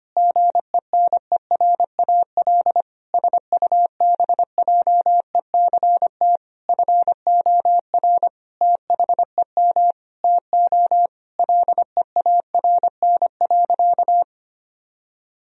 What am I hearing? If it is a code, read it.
Morse code, heard as a keyed tone, about 25 wpm, 700 Hz. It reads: GENERAL SUBJECT FOR THEM TO LEARN.
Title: Sentences from Top 1000 Words - Review Part 3 - 25wpm